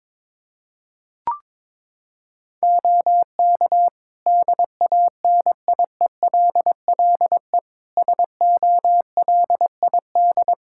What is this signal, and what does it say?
Morse code, heard as a keyed tone, about 22 wpm, 700 Hz. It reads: OK DANIELLE SOLID
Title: Practice QSO - Exchange 3 - More Info - 22wpm